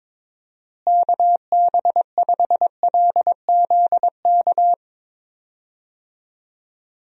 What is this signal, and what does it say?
Morse code, heard as a keyed tone, about 22 wpm, 700 Hz. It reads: KB5LZK
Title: USA Call Signs - 6 Characters - Repeated 2x - 22wpm